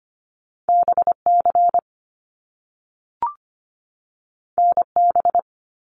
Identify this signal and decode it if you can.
Morse code, heard as a keyed tone, about 25 wpm, 700 Hz. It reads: BC NB
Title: Canadian Provinces and Territories 25wpm